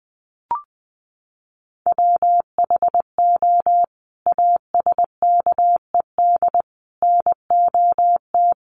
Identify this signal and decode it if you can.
Morse code, heard as a keyed tone, about 20 wpm, 700 Hz. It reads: WHO ASKED NOT
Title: Sets of 5 Words Encore2 20wpm